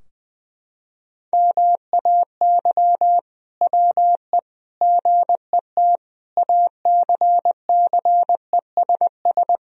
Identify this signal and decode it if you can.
Morse code, heard as a keyed tone, about 21 wpm, 700 Hz. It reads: MAY WE GET ACCESS
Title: Sentences from Top 1000 Words - Review Part 4 - 20wpm